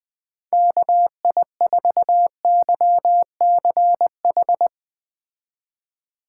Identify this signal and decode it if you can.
Morse code, heard as a keyed tone, about 20 wpm, 700 Hz. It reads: KI4YCH